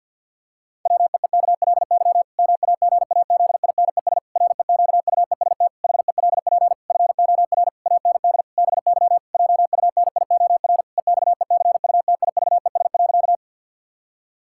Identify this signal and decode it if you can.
Morse code, heard as a keyed tone, about 50 wpm, 700 Hz. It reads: WEEKLY MAGAZINES REQUEST HELP FOR AND BY JUNIOR EXECUTIVES.
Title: Pangram Challenge v2 - 50wpm